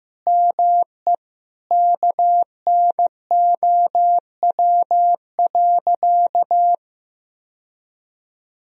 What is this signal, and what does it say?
Morse code, heard as a keyed tone, about 15 wpm, 700 Hz. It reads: ME KNOW.
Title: Sentences from Top 1000 Words - Review Part 2 - 15wpm